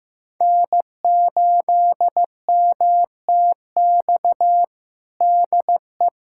Morse code, heard as a keyed tone, about 15 wpm, 700 Hz. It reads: N8MTX DE
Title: USA Callsign DE Callsign - 5 Characters - 15wpm